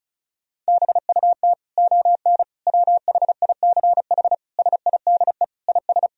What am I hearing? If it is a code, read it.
Morse code, heard as a keyed tone, about 36 wpm, 700 Hz. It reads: BUT ON WHICH SIDE IS
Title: Sentences from Top 1000 Words - Review Part 1 - 35wpm